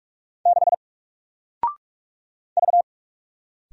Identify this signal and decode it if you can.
Morse code, heard as a keyed tone, about 44 wpm, 700 Hz. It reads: 6 V
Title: Single Character (Letter:Number:Punctuation:Prosign) 45wpm